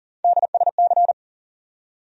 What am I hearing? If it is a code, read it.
Morse code, heard as a keyed tone, about 40 wpm, 700 Hz. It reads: DSC
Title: USA Call Sign Postfixes - 3 Letters 40wpm